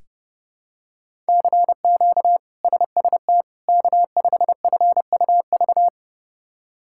Morse code, heard as a keyed tone, about 30 wpm, 700 Hz. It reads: CQ SST K5FUV
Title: K1USN SST QSO Practice - 30wpm